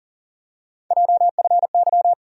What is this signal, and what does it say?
Morse code, heard as a keyed tone, about 40 wpm, 700 Hz. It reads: JFY
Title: USA Call Sign Postfixes - 3 Letters 40wpm